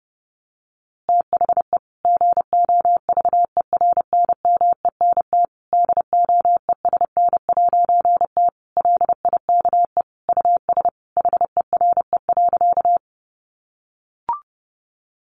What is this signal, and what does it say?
Morse code, heard as a keyed tone, about 30 wpm, 700 Hz. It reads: THE GOVERNMENT DOESN'T LIKE US HERE.
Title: A Sound Of Thunder 30wpm